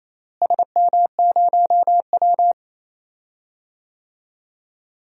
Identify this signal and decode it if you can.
Morse code, heard as a keyed tone, about 28 wpm, 700 Hz. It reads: SM0W
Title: DXCC-100 Call Signs - 4 Characters - Repeated 2x - 28wpm